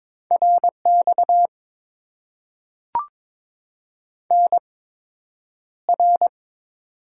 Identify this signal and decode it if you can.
Morse code, heard as a keyed tone, about 22 wpm, spaced as for 9 wpm, 700 Hz. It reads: RX NR
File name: Top 90 QSO Elements - 8x Character Spacing - 22wpm